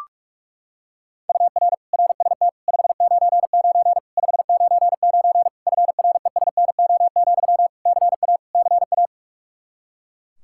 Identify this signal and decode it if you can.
Morse code, heard as a keyed tone, about 41 wpm, 700 Hz. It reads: UR RST 599 599 FRESNO, CA CA
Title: Practice QSO - Exchange 1 - Beginning - 45wpm